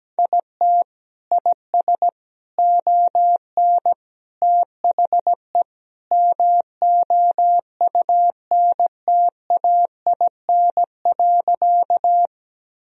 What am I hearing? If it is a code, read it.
Morse code, heard as a keyed tone, about 17 wpm, 700 Hz. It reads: IT IS ON THE MOUNTAIN.
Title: Sentences from Top 1000 Words - Review Part 1 - 17wpm